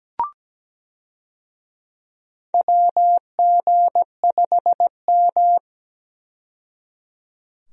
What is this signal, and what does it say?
Morse code, heard as a keyed tone, about 17 wpm, 700 Hz. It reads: WG5M
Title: USA Call Signs - 4 Characters - 17wpm